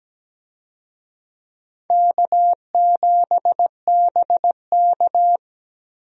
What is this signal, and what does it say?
Morse code, heard as a keyed tone, about 17 wpm, 700 Hz. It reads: K7BK
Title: USA Call Signs - 4 Characters - Repeated 3x - 17wpm